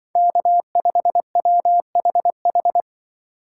Morse code, heard as a keyed tone, about 24 wpm, 700 Hz. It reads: K5WHH
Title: Speed Racing - USA 1x3 Call Signs 20wpm